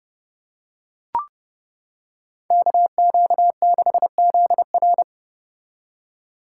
Morse code, heard as a keyed tone, about 31 wpm, 700 Hz. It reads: KQ6ZR